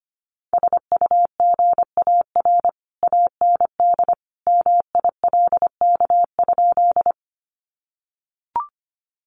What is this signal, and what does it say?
Morse code, heard as a keyed tone, about 25 wpm, 700 Hz. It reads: SUGAR AND MILK?
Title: Sentences from Top 1500 Words 25wpm